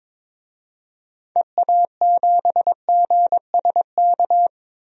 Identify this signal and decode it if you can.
Morse code, heard as a keyed tone, about 21 wpm, 700 Hz. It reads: EA7GSK
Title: DXCC-100 Call Signs - 6 Characters - Repeated 3x - 22wpm